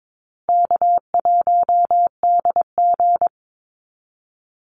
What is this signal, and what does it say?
Morse code, heard as a keyed tone, about 22 wpm, 700 Hz. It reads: K1DG
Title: Call Signs Repeated 2x - 22wpm